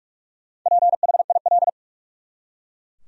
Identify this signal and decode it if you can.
Morse code, heard as a keyed tone, about 48 wpm, 700 Hz. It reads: PHIL
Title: USA Common Names - Max 5 Letters - 45wpm